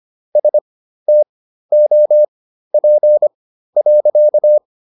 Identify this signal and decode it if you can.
Morse code, heard as a keyed tone, about 25 wpm, spaced as for 13 wpm, 600 Hz. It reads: STOP.